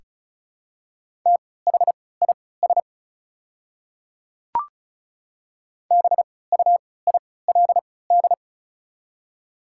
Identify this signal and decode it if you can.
Morse code, heard as a keyed tone, about 36 wpm, spaced as for 20 wpm, 690 Hz. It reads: THIS BUILD
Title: 3x Character Spacing - Top 200 Words - 35wpm